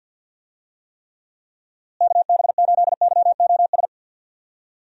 Morse code, heard as a keyed tone, about 50 wpm, 700 Hz. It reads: KB8YOS
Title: USA Call Signs - 6 Characters - Repeated 2x - 50wpm